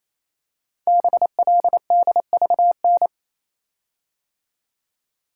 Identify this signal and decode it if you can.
Morse code, heard as a keyed tone, about 29 wpm, 700 Hz. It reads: BLDVN
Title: Code Groups - 5 Letter:Numbers - 28wpm